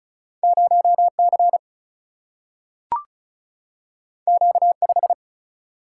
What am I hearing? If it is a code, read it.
Morse code, heard as a keyed tone, about 36 wpm, 700 Hz. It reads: 0C Q5
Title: Single-Letter Single-Number 35wpm